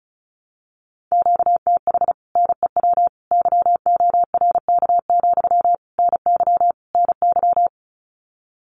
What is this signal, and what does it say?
Morse code, heard as a keyed tone, about 34 wpm, 700 Hz. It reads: QTH NEW YORK, NY NY
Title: Common QSO Phrases Encore 35wpm